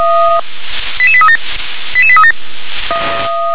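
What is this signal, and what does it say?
Five-tone selective call (ZVEI), heard as quick runs of five tones, tones of about 70 ms. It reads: 88837 88837
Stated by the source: ZVEI five-tone selective call received on SDR.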